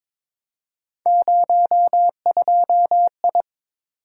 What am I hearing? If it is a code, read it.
Morse code, heard as a keyed tone, about 22 wpm, 700 Hz. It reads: 02I
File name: practice / 22wpm / Three Letter-Number Combinations - 22wpm